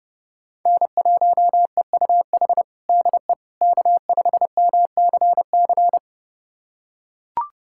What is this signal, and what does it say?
Morse code, heard as a keyed tone, about 30 wpm, 700 Hz. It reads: N1EUH DE K5MCC
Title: USA Callsign DE Callsign - 5 Characters - 30wpm